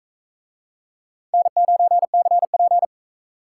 Morse code, heard as a keyed tone, about 40 wpm, 700 Hz. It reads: N9CP